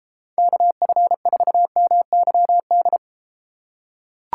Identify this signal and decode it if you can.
Morse code, heard as a keyed tone, about 32 wpm, 700 Hz. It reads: KF4MYD